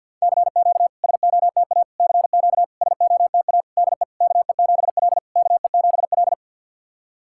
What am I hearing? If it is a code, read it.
Morse code, heard as a keyed tone, about 50 wpm, 690 Hz. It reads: CQ SOTA CQ SOTA DE KE7L KE7L
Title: SOTA QSO Practice 50wpm